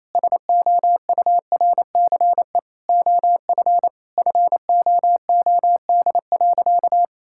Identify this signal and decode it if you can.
Morse code, heard as a keyed tone, about 28 wpm, 700 Hz. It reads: SOURCE OF FOOD.
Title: Sentences from Top 1000 Words - Review Part 5 - 28wpm